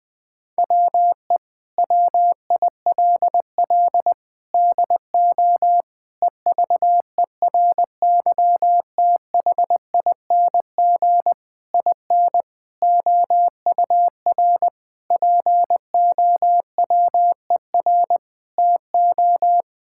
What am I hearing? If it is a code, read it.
Morse code, heard as a keyed tone, about 20 wpm, 700 Hz. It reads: WE WILL DO EVERYTHING IN OUR POWER TO
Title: Sentences with 8-Word Phrases - Spoken Beginning - 20wpm